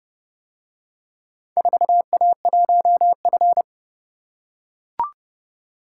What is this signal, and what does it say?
Morse code, heard as a keyed tone, about 30 wpm, 700 Hz. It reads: 4A1F